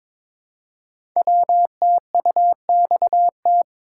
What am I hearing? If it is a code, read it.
Morse code, heard as a keyed tone, about 22 wpm, 700 Hz. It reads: WTUXT
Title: Code Groups - 5 Letters - 22wpm